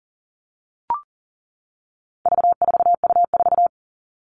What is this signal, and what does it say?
Morse code, heard as a keyed tone, about 40 wpm, 700 Hz. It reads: V4U4
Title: All Cut Numbers - 4-Digit 40wpm